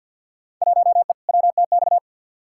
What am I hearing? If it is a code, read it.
Morse code, heard as a keyed tone, about 53 wpm, 700 Hz. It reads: 1E WTX
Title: Field Day Exchanges - 50wpm